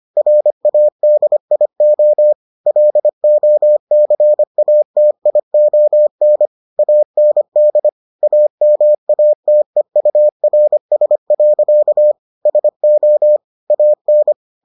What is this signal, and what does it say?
Morse code, heard as a keyed tone, about 25 wpm, 600 Hz. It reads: RADIO LOCATION AND AMATEURS. SO AN